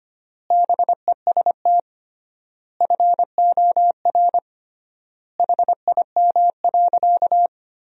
Morse code, heard as a keyed tone, about 25 wpm, 700 Hz. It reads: BEST FOR HIM.
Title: 3x Word-Spacing - Sentences from Top 500 Words - 25wpm